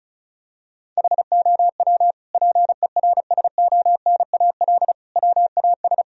Morse code, heard as a keyed tone, about 34 wpm, 700 Hz. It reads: HOW PERSONAL WAS
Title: Sentences from Top 1000 Words - Review Part 4 - 35wpm